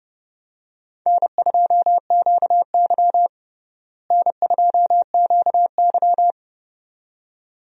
Morse code, heard as a keyed tone, about 30 wpm, 700 Hz. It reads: N2QY N2QY
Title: USA Call Signs - 4 Characters - Repeated 2x - 30wpm